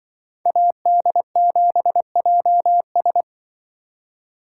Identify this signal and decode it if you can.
Morse code, heard as a keyed tone, about 24 wpm, 700 Hz. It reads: AD7JS
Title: Speed Racing - USA 2x2 Call Signs 20wpm